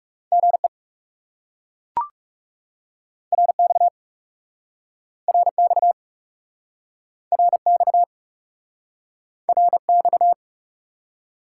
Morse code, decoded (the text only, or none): GE RX RX RX RX